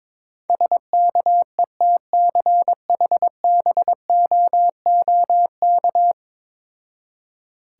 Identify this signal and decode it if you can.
Morse code, heard as a keyed tone, about 22 wpm, 700 Hz. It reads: SKETCHBOOK
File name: Difficult Words with Hints - 22wpm